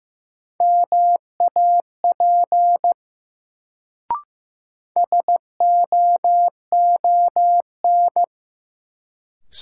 Morse code, heard as a keyed tone, about 15 wpm, 700 Hz. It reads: MAP SOON
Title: Sentences from Top 400 Words - New Word Drill - 15wpm